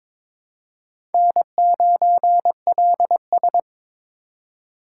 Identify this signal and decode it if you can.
Morse code, heard as a keyed tone, about 22 wpm, 700 Hz. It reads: N9LS
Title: USA Call Signs - 4 Characters - 3x Character Spacing - 22wpm